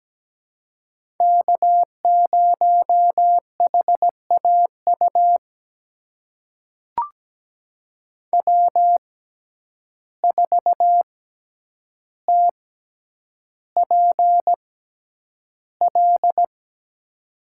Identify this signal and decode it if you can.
Morse code, heard as a keyed tone, about 17 wpm, spaced as for 9 wpm, 700 Hz. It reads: K0HAU W4TPL
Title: USA Call Signs - 5 Characters - 6x Character Spacing - 17wpm